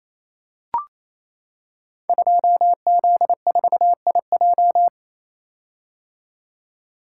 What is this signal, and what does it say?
Morse code, heard as a keyed tone, about 28 wpm, 700 Hz. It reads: 2Z4IJ
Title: Code Groups - 5 Letter:Numbers - 28wpm